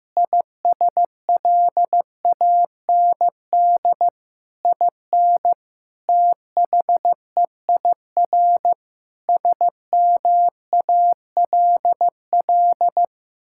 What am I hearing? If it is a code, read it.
Morse code, heard as a keyed tone, about 15 wpm, 700 Hz. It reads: ISLAND IN THEIR SMALL
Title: Sentences from Top 1000 Words - Review Part 6 - 15wpm